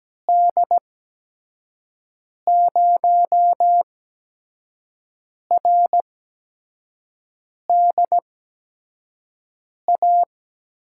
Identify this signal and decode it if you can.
Morse code, heard as a keyed tone, about 17 wpm, spaced as for 5 wpm, 700 Hz. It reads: D0RDA
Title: USA Call Signs - 6 Characters - 8x Character Spacing - 17wpm